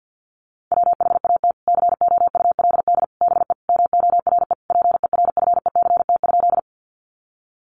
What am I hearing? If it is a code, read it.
Morse code, heard as a keyed tone, about 50 wpm, 700 Hz. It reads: WHAT COULD BE MORE PERFECT?
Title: Sentences from Top 1000 Words 50wpm